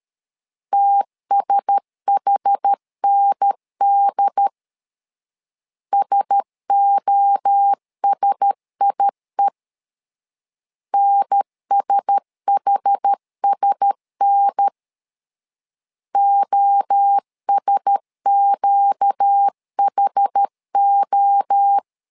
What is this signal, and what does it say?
Morse code, heard as a keyed tone, about 13 wpm, 800 Hz. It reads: TSHND SOSIE NSHSN OSQHO
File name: LEZIONI / 4 Q1S 12wpm 1802